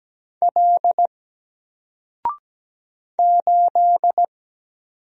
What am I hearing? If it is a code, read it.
Morse code, heard as a keyed tone, about 17 wpm, 700 Hz. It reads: L 8
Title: Single Letter-Number 17wpm